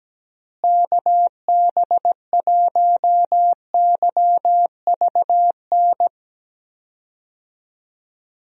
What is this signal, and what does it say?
Morse code, heard as a keyed tone, about 17 wpm, 700 Hz. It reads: KB1YVN
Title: USA Call Signs - 6 Characters - Repeated 2x - 17wpm